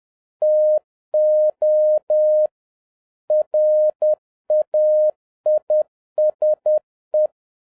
Morse code, heard as a keyed tone, about 10 wpm, 610 Hz. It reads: TO RAISE